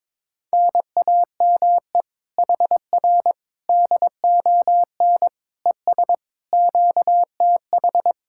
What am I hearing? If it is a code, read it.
Morse code, heard as a keyed tone, about 22 wpm, 700 Hz. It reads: NAME HR DON ES QTH
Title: Speed-Racing Common QSO Phrases - 22wpm